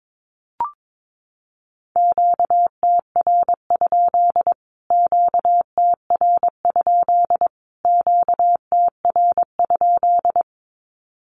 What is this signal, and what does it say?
Morse code, heard as a keyed tone, about 22 wpm, 700 Hz. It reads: QTR? QTR? QTR?
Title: Q-Codes Repeated 3x 22wpm